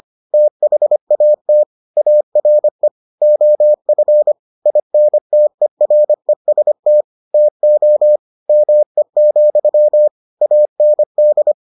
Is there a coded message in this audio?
Yes: Morse code from 0.3 to 11.5 s, heard as a keyed tone, about 25 wpm, 600 Hz. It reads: THAT ARE OF INTEREST TO ME, AND